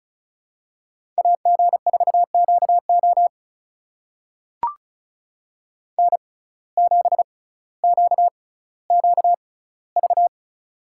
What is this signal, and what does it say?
Morse code, heard as a keyed tone, about 34 wpm, spaced as for 18 wpm, 700 Hz. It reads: AG4QO N7QQV
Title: USA Call Signs - 5 Characters - 6x Character Spacing - 35wpm